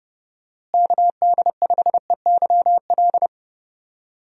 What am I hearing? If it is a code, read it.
Morse code, heard as a keyed tone, about 30 wpm, 700 Hz. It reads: KD5EYL
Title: USA Call Signs - 6 Characters - Repeated 3x - 30wpm